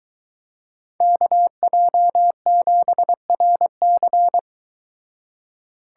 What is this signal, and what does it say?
Morse code, heard as a keyed tone, about 23 wpm, 700 Hz. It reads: KJ7RC